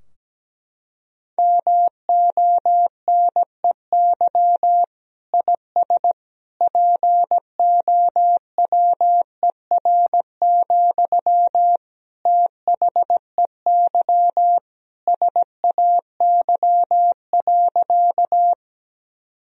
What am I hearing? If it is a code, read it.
Morse code, heard as a keyed tone, about 17 wpm, 700 Hz. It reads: MONEY IS POWER, THEY SAY.
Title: Sentences from Top 1000 Words - Review Part 2 - 17wpm